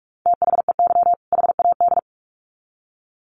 Morse code, heard as a keyed tone, about 44 wpm, 700 Hz. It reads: THEY HAD
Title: Sets of 2 Words 45wpm